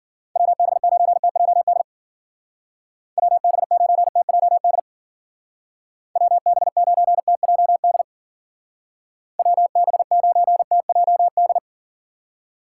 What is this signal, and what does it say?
Morse code, heard as a keyed tone, about 50 wpm, 700 Hz. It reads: WB9TJD WB9TJD WB9TJD WB9TJD
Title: Speed Racing - USA Call Signs 40wpm